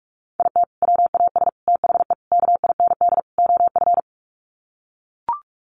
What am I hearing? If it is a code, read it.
Morse code, heard as a keyed tone, about 44 wpm, 700 Hz. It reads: IT WAS THE KIND OF
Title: Sets of 5 Words Encore2 45wpm